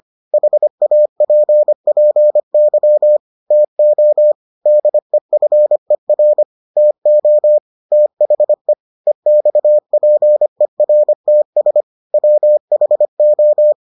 Morse code, heard as a keyed tone, about 25 wpm, 600 Hz. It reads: HAPPY TO DEFER TO THE EXPERTS WHO